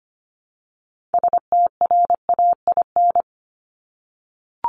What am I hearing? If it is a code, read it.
Morse code, heard as a keyed tone, about 25 wpm, 700 Hz. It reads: STRAIN
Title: Top 1751-2000 Words 25wpm